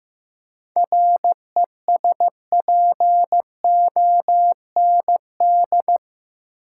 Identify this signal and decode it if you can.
Morse code, heard as a keyed tone, about 16 wpm, 700 Hz. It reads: RESPOND
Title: Seven Letter Words - 15wpm